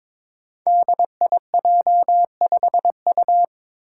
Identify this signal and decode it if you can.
Morse code, heard as a keyed tone, about 22 wpm, 700 Hz. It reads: DIJ5U
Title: Five Challenging Letter-Number Combinations 22wpm